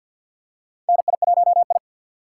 Morse code, heard as a keyed tone, about 46 wpm, 700 Hz. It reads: NI1I
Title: USA Call Signs - 4 Characters - 3x Character Spacing - 50wpm